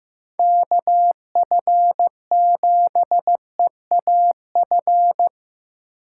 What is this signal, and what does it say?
Morse code, heard as a keyed tone, about 15 wpm, 700 Hz. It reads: KF7EAF
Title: USA Call Signs - 4x Character Spacing - 15wpm